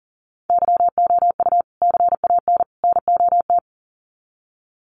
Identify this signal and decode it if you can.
Morse code, heard as a keyed tone, about 40 wpm, 700 Hz. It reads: YOU CAN NOT
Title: Sets of 3 Words 40wpm